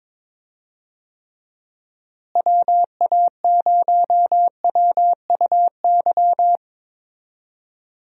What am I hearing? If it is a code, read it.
Morse code, heard as a keyed tone, about 23 wpm, 700 Hz. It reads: WA0WUY
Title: USA Call Signs - 6 Characters - Repeated 3x - 22wpm